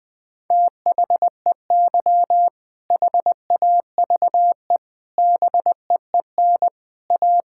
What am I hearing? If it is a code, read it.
Morse code, heard as a keyed tone, about 21 wpm, 700 Hz. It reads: THEY HAVE BEEN A
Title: Sentences with 3-Word Phrases - 20wpm